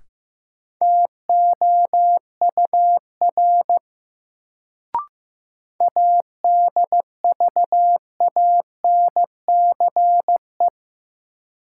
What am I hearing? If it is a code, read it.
Morse code, heard as a keyed tone, about 15 wpm, 700 Hz. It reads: TOUR ADVANCE
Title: Sentences from Top 1400 Words - New Word Drill Alt - 15wpm